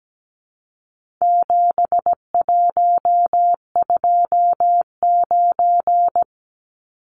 Morse code, heard as a keyed tone, about 17 wpm, 700 Hz. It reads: 7129
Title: Numbers - 4-Digit 17wpm